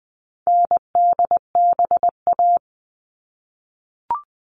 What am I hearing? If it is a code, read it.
Morse code, heard as a keyed tone, about 20 wpm, 700 Hz. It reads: NDBA
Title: All Cut Numbers - 4-Digit 20wpm